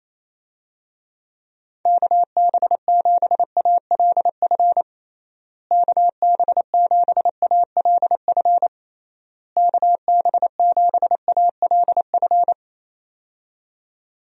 Morse code, heard as a keyed tone, about 28 wpm, 700 Hz. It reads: KB7ALF KB7ALF KB7ALF
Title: USA Call Signs - 6 Characters - Repeated 3x - 28wpm